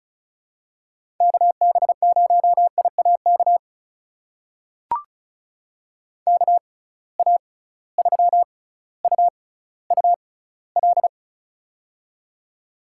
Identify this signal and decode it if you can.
Morse code, heard as a keyed tone, about 36 wpm, spaced as for 18 wpm, 700 Hz. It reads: KB0IAK KA3UUL
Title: USA Call Signs - 6x Character Spacing - 35wpm